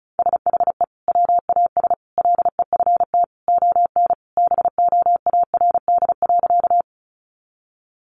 Morse code, heard as a keyed tone, about 34 wpm, 700 Hz. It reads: SHE WAS LEFT ON BOARD.
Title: Sentences from Top 700 Words 35wpm